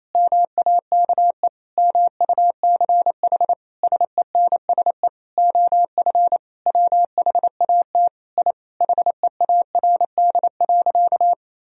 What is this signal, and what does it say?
Morse code, heard as a keyed tone, about 28 wpm, 700 Hz. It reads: MAKE MUCH SENSE OF WHAT I HEARD.
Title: Sentences from Top 1000 Words - Review Part 4 - 28wpm